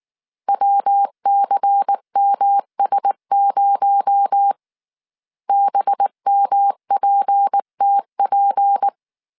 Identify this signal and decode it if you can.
Morse code, heard as a keyed tone, about 19 wpm, 800 Hz. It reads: WCMS0 BMPTP
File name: CW / DET / 1 6 1500C S C 18wpm 2356